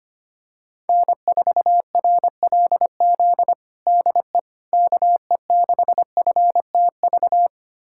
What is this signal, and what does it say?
Morse code, heard as a keyed tone, about 25 wpm, 700 Hz. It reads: N4RLZ DE KE6FTV
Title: USA Callsign DE Callsign 25wpm